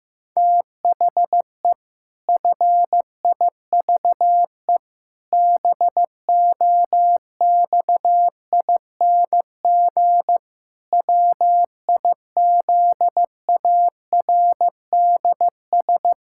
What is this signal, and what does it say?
Morse code, heard as a keyed tone, about 15 wpm, 700 Hz. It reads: THE FIVE BOXING WIZARDS
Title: Pangram Challenge v2 - 15wpm